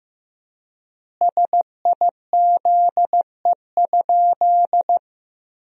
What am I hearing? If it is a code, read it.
Morse code, heard as a keyed tone, about 16 wpm, 700 Hz. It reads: SIZE?
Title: 4x Word-Spacing - Sentences from Top 500 Words - 15wpm